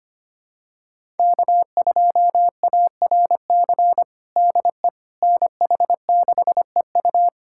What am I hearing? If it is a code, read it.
Morse code, heard as a keyed tone, about 25 wpm, 700 Hz. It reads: K2ARC DE NH6EU
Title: USA Callsign DE Callsign - 5 Characters - 25wpm